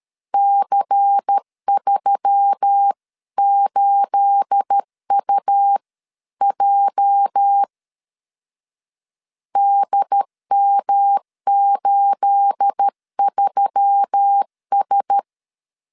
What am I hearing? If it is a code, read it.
Morse code, heard as a keyed tone, about 13 wpm, 800 Hz. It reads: C38UJ DM83S
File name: CW / DET / 1 9 1500C S B 12wpm 3819